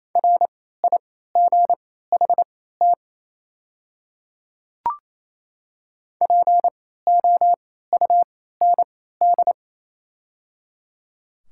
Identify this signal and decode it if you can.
Morse code, heard as a keyed tone, about 28 wpm, spaced as for 16 wpm, 700 Hz. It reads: RIGHT POUND
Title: 3x Character Spacing - Top 500 Words - 28wpm